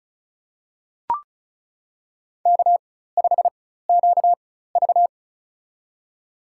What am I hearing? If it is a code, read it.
Morse code, heard as a keyed tone, about 34 wpm, spaced as for 16 wpm, 700 Hz. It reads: K5QV
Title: USA Call Signs - 4x Character Spacing - 35wpm